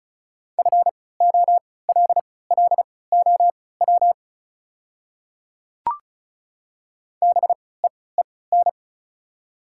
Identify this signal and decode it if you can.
Morse code, heard as a keyed tone, about 36 wpm, spaced as for 20 wpm, 690 Hz. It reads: FOLLOW BEEN
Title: 3x Character Spacing - Top 300 Words - 35wpm